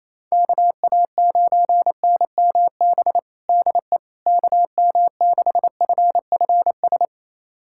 Morse code, heard as a keyed tone, about 28 wpm, 700 Hz. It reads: KA9NMB DE KM6FFS
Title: USA Callsign DE Callsign 28wpm